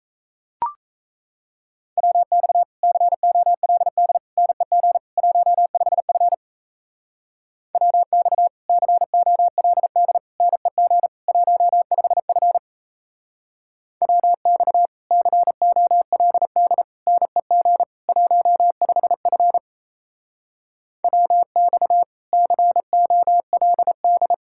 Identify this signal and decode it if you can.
Morse code, heard as a keyed tone, about 34 wpm, 700 Hz. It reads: WX COLD NEG 15F WX COLD NEG 15F WX COLD NEG 15F WX COLD
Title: Speed-Racing Common QSO Phrases - 28wpm